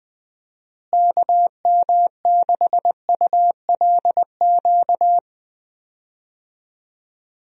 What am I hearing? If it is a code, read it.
Morse code, heard as a keyed tone, about 20 wpm, 700 Hz. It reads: KM6ULQ
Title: USA Call Signs - 6 Characters - Repeated 3x - 20wpm